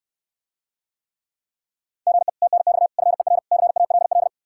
Morse code, heard as a keyed tone, about 69 wpm, 700 Hz. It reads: GE TNX FER CALL